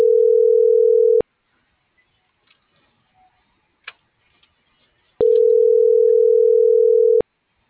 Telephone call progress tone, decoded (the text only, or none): ringback tone